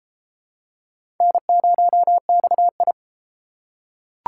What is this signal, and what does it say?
Morse code, heard as a keyed tone, about 28 wpm, 700 Hz. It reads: N0XI